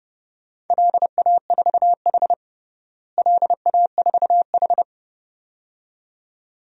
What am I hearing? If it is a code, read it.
Morse code, heard as a keyed tone, about 30 wpm, 700 Hz. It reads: LA4H LA4H
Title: DXCC-100 Call Signs - 4 Characters - Repeated 2x - 30wpm